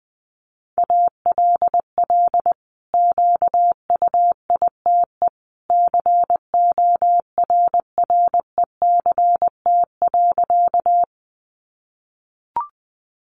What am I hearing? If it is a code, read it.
Morse code, heard as a keyed tone, about 20 wpm, 700 Hz. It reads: ALL QUITE CORRECT.
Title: Sentences from Top 700 Words 20wpm